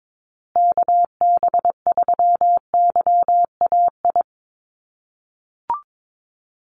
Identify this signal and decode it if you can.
Morse code, heard as a keyed tone, about 22 wpm, 700 Hz. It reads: KB3YAI
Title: USA Call Signs Repeated 2x 22wpm